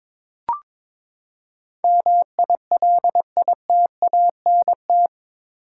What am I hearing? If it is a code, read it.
Morse code, heard as a keyed tone, about 22 wpm, 700 Hz. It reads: MILITANT
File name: practice / 22wpm / Eight Letter Words - 22wpm